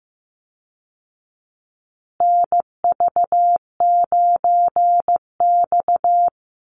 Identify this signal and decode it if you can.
Morse code, heard as a keyed tone, about 15 wpm, 700 Hz. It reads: NV9X
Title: Call Signs Repeated 2x - 15wpm